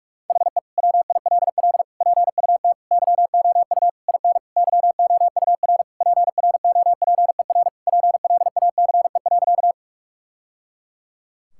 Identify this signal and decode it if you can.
Morse code, heard as a keyed tone, about 45 wpm, 690 Hz. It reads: HE WILL PUT YOU IN YOUR PROPER PLACE.
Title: Sentences from Top 1000 Words - Review Part 5 - 45wpm